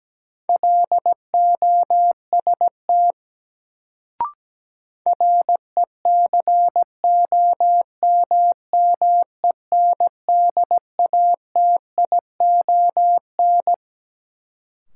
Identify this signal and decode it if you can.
Morse code, heard as a keyed tone, about 17 wpm, 700 Hz. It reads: LOST RECOMMENDATION
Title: Sentences from Top 1400 Words - New Word Drill - 17wpm